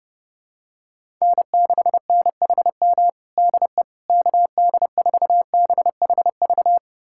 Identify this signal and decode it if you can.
Morse code, heard as a keyed tone, about 30 wpm, 700 Hz. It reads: N6NHM DE KD4BHV
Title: USA Callsign DE Callsign 30wpm